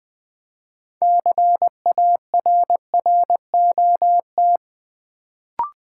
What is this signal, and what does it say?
Morse code, heard as a keyed tone, about 20 wpm, 700 Hz. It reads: CARROT
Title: Build A Word - 2 Letters At A Time - 6-Letter Words - 20wpm